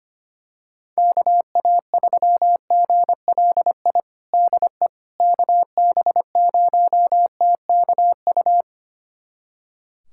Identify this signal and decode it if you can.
Morse code, heard as a keyed tone, about 25 wpm, 700 Hz. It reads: KA3GLI DE KB0TKU
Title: USA Callsign DE Callsign - 6 Characters - 25wpm